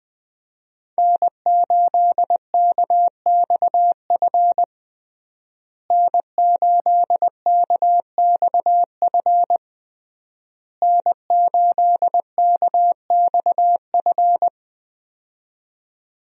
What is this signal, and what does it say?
Morse code, heard as a keyed tone, about 20 wpm, 700 Hz. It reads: N8KXF N8KXF N8KXF
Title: USA Call Signs - 5 Characters - Repeated 3x - 20wpm